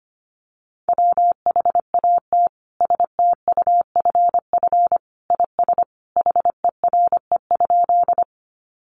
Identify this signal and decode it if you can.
Morse code, heard as a keyed tone, about 25 wpm, 700 Hz. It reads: WHAT STUFF IS HERE?
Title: Sentences from Top 1400 Words 25wpm